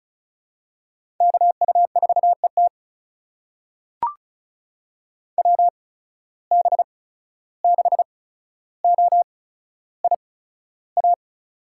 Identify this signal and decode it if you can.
Morse code, heard as a keyed tone, about 36 wpm, spaced as for 14 wpm, 700 Hz. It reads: KU4ET WB6OIA
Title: USA Call Signs - 8x Character Spacing - 35wpm